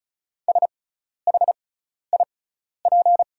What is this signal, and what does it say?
Morse code, heard as a keyed tone, about 36 wpm, spaced as for 12 wpm, 700 Hz. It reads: SHIP
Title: 6x Character Spacing - Top 500 Words - 35wpm